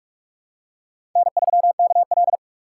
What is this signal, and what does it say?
Morse code, heard as a keyed tone, about 44 wpm, 700 Hz. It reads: N3KL
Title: USA Call Signs - 6x Character Spacing - 45wpm